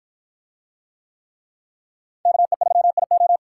Morse code, heard as a keyed tone, about 53 wpm, 700 Hz. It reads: KE3IO